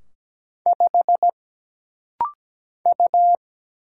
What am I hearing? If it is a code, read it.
Morse code, heard as a keyed tone, about 17 wpm, 700 Hz. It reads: 5 U
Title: Single Letter-Number Alt - Rapid-Fire 17wpm